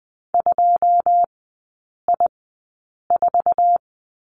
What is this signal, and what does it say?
Morse code, heard as a keyed tone, about 20 wpm, spaced as for 8 wpm, 700 Hz. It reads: 2I4
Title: 100 Lesson 17e - Letter u - Three Characters 20wpm